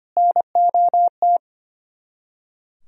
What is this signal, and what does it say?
Morse code, heard as a keyed tone, about 33 wpm, 700 Hz. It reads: NOT